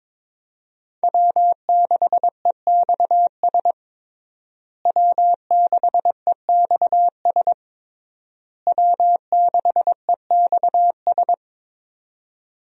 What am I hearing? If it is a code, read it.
Morse code, heard as a keyed tone, about 21 wpm, 700 Hz. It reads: W6EXS W6EXS W6EXS
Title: USA Call Signs - 5 Characters - Repeated 3x - 22wpm